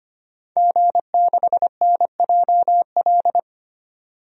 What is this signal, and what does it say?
Morse code, heard as a keyed tone, about 25 wpm, 700 Hz. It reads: G6NJL
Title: DXCC-100 Call Signs - 5 Characters - 25wpm